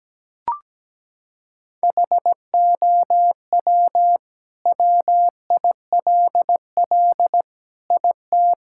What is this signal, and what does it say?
Morse code, heard as a keyed tone, about 17 wpm, 700 Hz. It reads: HOW WILL IT
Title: Sentences from Top 1000 Words - Review Part 6 - 17wpm